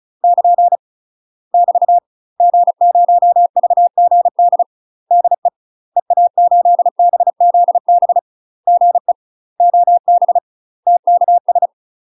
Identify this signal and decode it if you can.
Morse code, heard as a keyed tone, about 36 wpm, 700 Hz. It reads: <KN> <BT> G0VGD DE EA8BZB GE OB TKS
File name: UK MASTER QSOs TRAINING / 35WPM / UK MASTER QSOs 0020